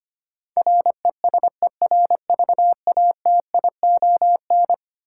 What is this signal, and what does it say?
Morse code, heard as a keyed tone, about 25 wpm, 700 Hz. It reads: RESERVATION
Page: morse-code-ninja-practice-25wpm